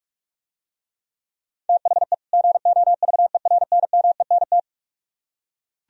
Morse code, heard as a keyed tone, about 47 wpm, 700 Hz. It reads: THE GOVERNMENT